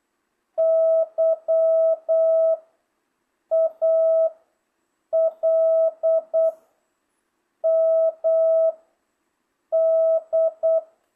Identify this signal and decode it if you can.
Morse code, heard as a keyed tone, about 8 wpm, 650 Hz. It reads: Y A L M D